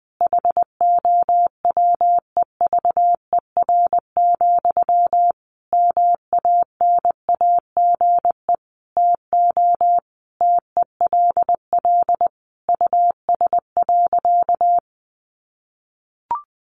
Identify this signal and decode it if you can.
Morse code, heard as a keyed tone, about 20 wpm, 700 Hz. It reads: HOWEVER, MANAGE TO TELL US.
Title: Sentences from Top 1400 Words 20wpm